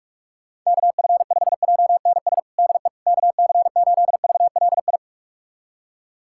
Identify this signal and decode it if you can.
Morse code, heard as a keyed tone, about 46 wpm, 700 Hz. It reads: KF5JNS DE KC8VLI
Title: USA Callsign DE Callsign - 6 Characters - 45wpm